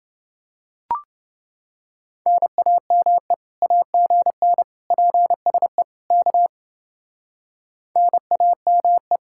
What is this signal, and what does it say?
Morse code, heard as a keyed tone, about 29 wpm, 700 Hz. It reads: NAME AGN PSE K NAME